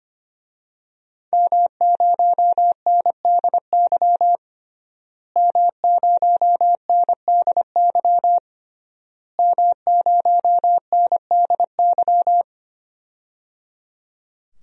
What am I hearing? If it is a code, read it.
Morse code, heard as a keyed tone, about 25 wpm, 700 Hz. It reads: M0NDY M0NDY M0NDY
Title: DXCC-100 Call Signs - 5 Characters - Repeated 3x - 25wpm